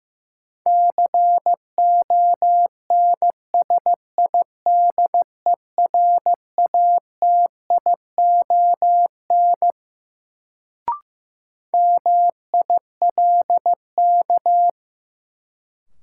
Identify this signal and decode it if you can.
Morse code, heard as a keyed tone, about 15 wpm, 700 Hz. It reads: CONSIDERATION MILK
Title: Sentences from Top 1500 Words - New Word Drill - 15wpm